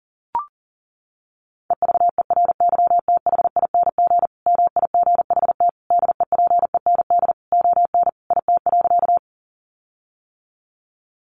Morse code, heard as a keyed tone, about 37 wpm, 700 Hz. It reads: EVERYTHING MIGHT DEPEND ON IT.
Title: Sentences from Top 1300 Words 40wpm